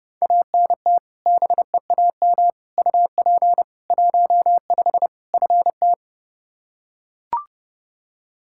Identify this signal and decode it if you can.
Morse code, heard as a keyed tone, about 30 wpm, 700 Hz. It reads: ANT BEAM UP 15 FT
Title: Practice QSO - Exchange 2 - Info - 30wpm